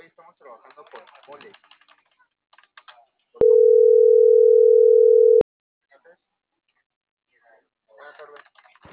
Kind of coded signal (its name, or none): telephone call progress tone